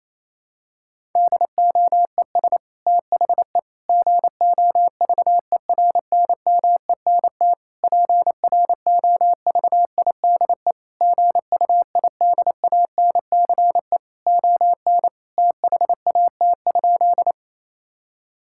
Morse code, heard as a keyed tone, about 28 wpm, 700 Hz. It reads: DOES THE GOVERNMENT PROVIDE GUIDANCE ON THAT?
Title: Sentences from Top 1600 Words 28wpm